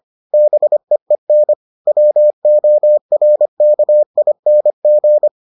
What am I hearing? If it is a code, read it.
Morse code, heard as a keyed tone, about 25 wpm, 600 Hz. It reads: BEEN WORKING